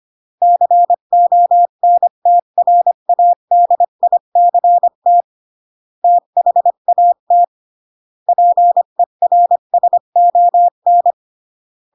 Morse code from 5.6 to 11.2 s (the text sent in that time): THAT PERSON